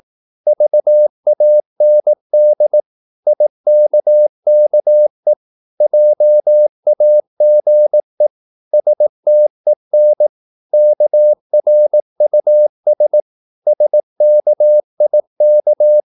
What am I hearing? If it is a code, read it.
Morse code, heard as a keyed tone, about 18 wpm, 600 Hz. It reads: VAND IKKE JAGE STEN KRUS SKIK
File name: Korte ord DK 18wpm